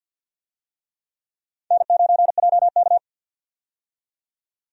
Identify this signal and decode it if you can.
Morse code, heard as a keyed tone, about 52 wpm, 700 Hz. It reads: N9JK
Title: USA Call Signs - 4 Characters - 50wpm